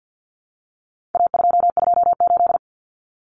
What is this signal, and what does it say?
Morse code, heard as a keyed tone, about 50 wpm, 700 Hz. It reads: A228
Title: Common Cut Numbers - 4-Digit 50wpm